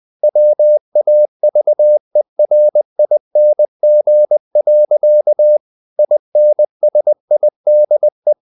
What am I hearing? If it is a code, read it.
Morse code, heard as a keyed tone, about 20 wpm, 600 Hz. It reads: WAVERING. INSIDE
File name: the old man and the sea part 2 20wpm